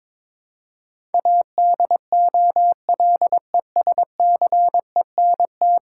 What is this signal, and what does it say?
Morse code, heard as a keyed tone, about 22 wpm, 700 Hz. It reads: ADOLESCENT
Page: morse-code-ninja-practice-22wpm